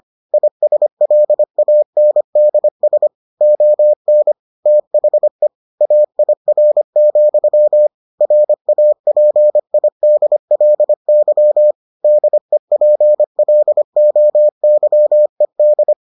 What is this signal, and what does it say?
Morse code, heard as a keyed tone, about 25 wpm, 600 Hz. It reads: ISLANDS ON THE AIR, RAPIDLY DEPLOYED